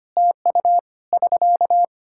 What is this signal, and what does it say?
Morse code, heard as a keyed tone, about 25 wpm, 700 Hz. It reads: TU <SK>
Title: Practice QSO - Exchange 4 - Ending - 25wpm